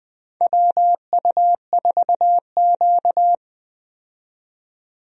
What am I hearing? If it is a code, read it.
Morse code, heard as a keyed tone, about 20 wpm, 700 Hz. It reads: WU4Q
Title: USA Call Signs - 4 Characters - 20wpm